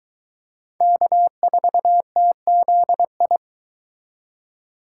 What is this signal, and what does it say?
Morse code, heard as a keyed tone, about 23 wpm, 700 Hz. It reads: K4TZI